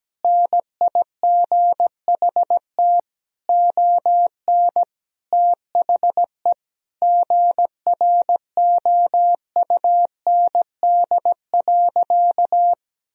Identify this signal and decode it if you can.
Morse code, heard as a keyed tone, about 17 wpm, 700 Hz. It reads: NIGHT ON THE GROUND.
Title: Sentences from Top 1000 Words - Review Part 2 - 17wpm